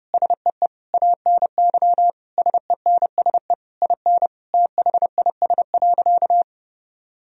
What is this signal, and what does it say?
Morse code, heard as a keyed tone, about 30 wpm, 700 Hz. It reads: SEE ANY SENSE IN THIS.
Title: Sentences from Top 1000 Words - Review Part 4 - 30wpm